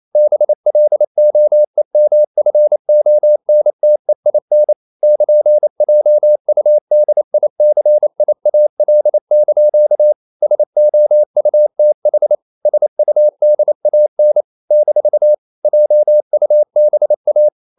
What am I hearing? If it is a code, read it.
Morse code, heard as a keyed tone, about 28 wpm, 600 Hz. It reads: BLOEMFONTEIN <KN>JUDICIAL) SOUTH SUDAN - JUBA
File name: CW / Practice / NationalCapitals28